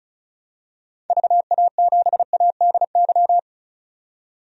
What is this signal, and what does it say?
Morse code, heard as a keyed tone, about 35 wpm, 700 Hz. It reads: VA7ADY
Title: DXCC-100 Call Signs - 6 Characters - 35wpm